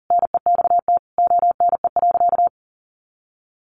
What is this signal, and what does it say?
Morse code, heard as a keyed tone, about 40 wpm, 700 Hz. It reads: NEXT ONE.
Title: Sentences from Top 400 Words 40wpm